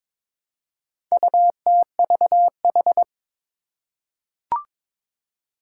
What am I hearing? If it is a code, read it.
Morse code, heard as a keyed tone, about 22 wpm, 700 Hz. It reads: UTVH
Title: Four Challenging Letter-Number Combinations 22wpm